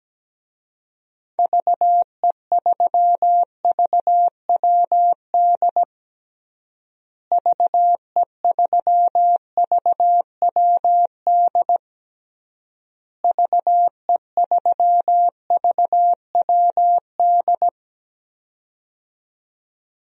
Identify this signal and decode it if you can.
Morse code, heard as a keyed tone, about 17 wpm, 700 Hz. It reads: VE3VWD VE3VWD VE3VWD
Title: DXCC-100 Call Signs - 6 Characters - Repeated 3x - 17wpm